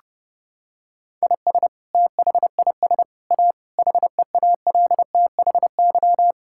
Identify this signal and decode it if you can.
Morse code, heard as a keyed tone, about 30 wpm, 700 Hz. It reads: IS THIS A HEALTHY
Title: Sentences from Top 1000 Words - Review Part 5 - 30wpm